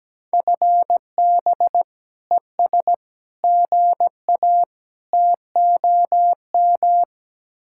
Morse code, heard as a keyed tone, about 17 wpm, 700 Hz. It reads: FB ES GA TOM
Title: Speed-Racing Common QSO Phrases - 17wpm